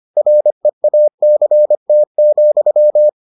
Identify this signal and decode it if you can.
Morse code, heard as a keyed tone, about 25 wpm, 600 Hz. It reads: REACT,